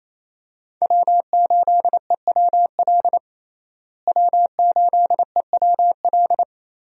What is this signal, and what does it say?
Morse code, heard as a keyed tone, about 27 wpm, 700 Hz. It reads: W8EWL W8EWL
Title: USA Call Signs - 5 Characters - Repeated 3x - 28wpm